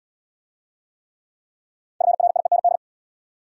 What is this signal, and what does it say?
Morse code, heard as a keyed tone, about 64 wpm, 700 Hz. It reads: FRIEND